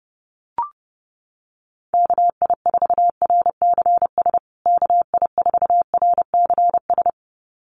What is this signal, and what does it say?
Morse code, heard as a keyed tone, about 30 wpm, 700 Hz. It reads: KI4RCS KI4RCS
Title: USA Call Signs Repeated 2x 30wpm